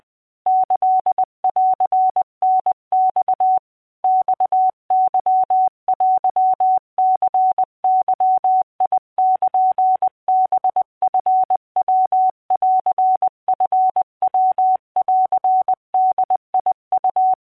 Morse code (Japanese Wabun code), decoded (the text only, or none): キンタマ マケテニゲルハチヤンチヤンボウ